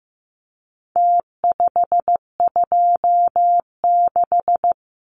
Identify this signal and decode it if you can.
Morse code, heard as a keyed tone, about 15 wpm, 700 Hz. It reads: T526
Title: Numbers - Contest Style - 4-Digit 15wpm